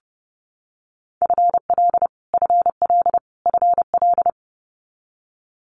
Morse code, heard as a keyed tone, about 30 wpm, 700 Hz. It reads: FL FL FL
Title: Multi-letter Phonemes Repeated 3x 30wpm